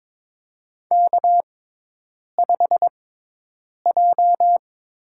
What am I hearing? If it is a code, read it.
Morse code, heard as a keyed tone, about 22 wpm, 700 Hz. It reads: K 5 J
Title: USA Call Signs - 6x Character Spacing - 22wpm